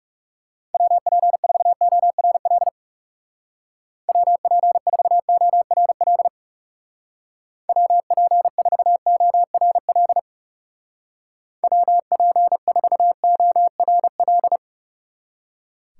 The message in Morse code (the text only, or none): WP4ORL WP4ORL WP4ORL WP4ORL